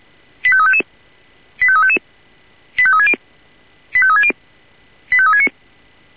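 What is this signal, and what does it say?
Five-tone selective call (ZVEI), heard as quick runs of five tones, tones of about 70 ms. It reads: 85355 85355 85360 85360 85369
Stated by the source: Signal received via SDR, ZVEI five-tone.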